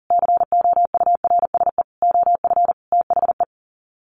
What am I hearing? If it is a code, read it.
Morse code, heard as a keyed tone, about 40 wpm, 700 Hz. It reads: COURSE OF THE
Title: Sets of 5 Words 40wpm